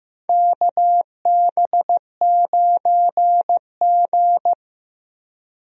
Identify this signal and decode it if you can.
Morse code, heard as a keyed tone, about 15 wpm, 700 Hz. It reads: KB9G
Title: USA Call Signs - 4 Characters - Repeated 2x - 15wpm